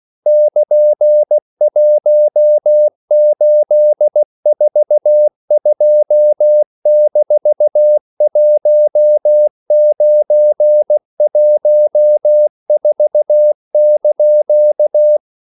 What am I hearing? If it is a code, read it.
Morse code, heard as a keyed tone, about 16 wpm, 600 Hz. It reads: <KN>1842-1914)
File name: CW / Practice / EnglishQuotations0216